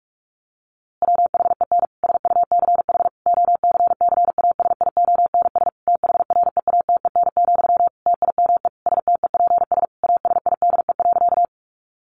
Morse code, heard as a keyed tone, about 44 wpm, 700 Hz. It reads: WHEN SUCH OCCASIONS THREATEN, TIME STEPS ASIDE.
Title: A Sound Of Thunder 45wpm